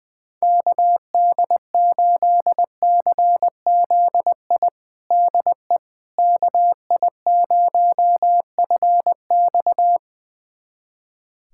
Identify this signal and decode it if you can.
Morse code, heard as a keyed tone, about 20 wpm, 700 Hz. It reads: KD8CZI DE KI0FX
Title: USA Callsign DE Callsign 20wpm